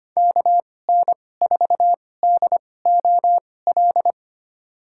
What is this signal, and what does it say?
Morse code, heard as a keyed tone, about 25 wpm, spaced as for 18 wpm, 700 Hz. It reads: KN4DOL
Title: USA Call Signs - 6 Characters - 2x Character Spacing - 25wpm